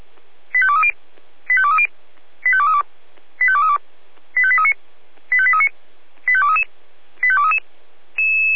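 Five-tone selective call (ZVEI), heard as quick runs of five tones, tones of about 70 ms. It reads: 85230 85230 85232 85232 85839 85839 85233 85233
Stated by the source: ZVEI five-tone call received on SDR.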